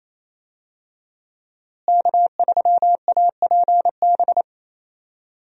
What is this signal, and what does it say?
Morse code, heard as a keyed tone, about 28 wpm, 700 Hz. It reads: K3APB
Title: USA Call Signs - 5 Characters - Repeated 2x - 28wpm